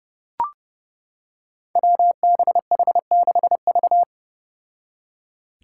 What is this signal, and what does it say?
Morse code, heard as a keyed tone, about 30 wpm, 700 Hz. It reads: WBH6V
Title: Five Challenging Letter-Number Combinations 30wpm